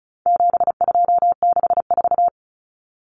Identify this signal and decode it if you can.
Morse code, heard as a keyed tone, about 34 wpm, 700 Hz. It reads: 7264
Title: Numbers - 4-Digit 35wpm